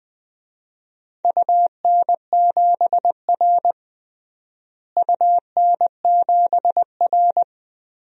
Morse code, heard as a keyed tone, about 20 wpm, 700 Hz. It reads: UN7R UN7R
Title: DXCC-100 Call Signs - 4 Characters - Repeated 2x - 20wpm